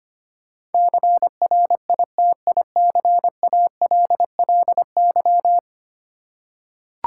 Morse code, heard as a keyed tone, about 25 wpm, 700 Hz. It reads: CRITICALLY
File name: Ten Letter Words - 25wpm